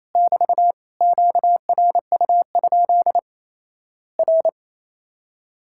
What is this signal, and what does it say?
Morse code, heard as a keyed tone, about 28 wpm, 700 Hz. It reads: <BT> QRU? R
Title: Worked All Europe DX Contest - QTC Exchange - 28wpm